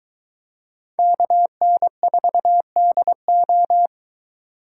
Morse code, heard as a keyed tone, about 22 wpm, 700 Hz. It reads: KN4DO